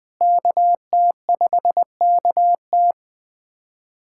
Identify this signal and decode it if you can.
Morse code, heard as a keyed tone, about 20 wpm, 700 Hz. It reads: KT5KT
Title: USA Callsign DE Callsign - 5 Characters - 20wpm